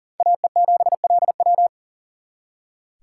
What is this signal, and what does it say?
Morse code, heard as a keyed tone, about 52 wpm, 700 Hz. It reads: AE7LW